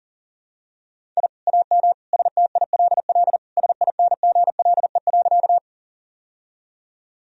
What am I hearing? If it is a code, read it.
Morse code, heard as a keyed tone, about 37 wpm, 700 Hz. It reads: I AM STILL SINGLE.
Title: Sentences from Top 1000 Words - Review Part 4 - 40wpm